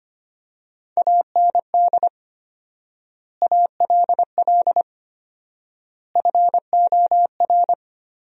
Morse code, heard as a keyed tone, about 25 wpm, 700 Hz. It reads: AND ALL FOR
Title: 4x Word-Spacing - Sentences from Top 100 Words - 25wpm